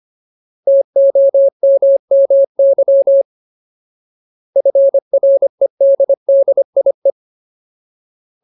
Morse code, heard as a keyed tone, about 25 wpm, 550 Hz. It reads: TOMMY FREDDIE
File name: Ultimate Word List / 25WPM Ultimate